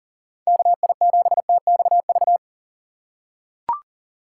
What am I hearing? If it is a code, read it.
Morse code, heard as a keyed tone, about 37 wpm, 700 Hz. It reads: KI7TXV
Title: USA Call Signs - 6 Characters - 6x Character Spacing - 40wpm